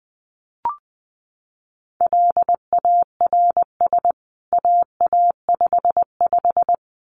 Morse code, heard as a keyed tone, about 20 wpm, 700 Hz. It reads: LARS AA55
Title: CWT Exchange 20wpm